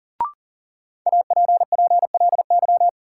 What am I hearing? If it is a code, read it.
Morse code, heard as a keyed tone, about 40 wpm, 700 Hz. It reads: APPLY
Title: Sentences from Top 1200 Words - New Word Drill Alt - 40wpm